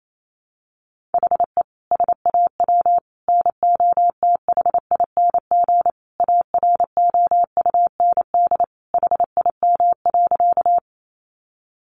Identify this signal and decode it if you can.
Morse code, heard as a keyed tone, about 28 wpm, 700 Hz. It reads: HE SAW NOTHING AROUND HIM.
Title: Sentences from Top 600 Words 28wpm